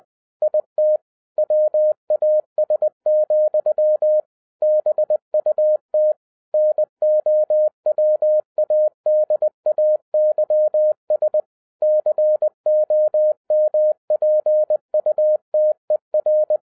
Morse code, heard as a keyed tone, about 20 wpm, 610 Hz. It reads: IT WAS, BUT NOWADAYS COMPUTER
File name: 2012-01-mp3 / n05-CAME BACK TO HAM RADIO AFTER 30 YEARS OF INTERVAL-20wpm